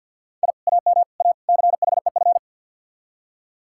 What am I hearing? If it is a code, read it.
Morse code, heard as a keyed tone, about 66 wpm, spaced as for 57 wpm, 700 Hz. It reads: I AM A CHEF
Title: Speed-Racing Common QSO Phrases - 50wpm